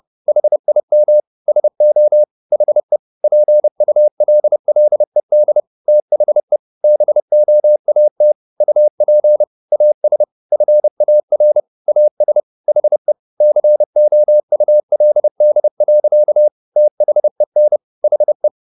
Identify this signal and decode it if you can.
Morse code, heard as a keyed tone, about 30 wpm, 600 Hz. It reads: HIM SO HE PULLED THE BOAT UP AS FAR AS HE COULD. THEN HE